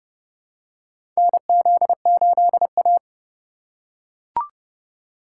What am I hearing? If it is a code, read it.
Morse code, heard as a keyed tone, about 30 wpm, 700 Hz. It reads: NZ8A
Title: USA Call Signs - 4 Characters - 4x Character Spacing - 30wpm